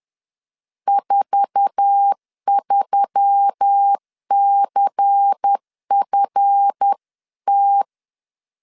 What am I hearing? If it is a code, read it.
Morse code, heard as a keyed tone, about 10 wpm, 800 Hz. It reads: 43CFT